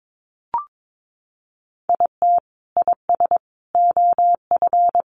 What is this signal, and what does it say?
Morse code, heard as a keyed tone, about 22 wpm, 700 Hz. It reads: IT IS OF
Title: Sentences from Top 1200 Words 22wpm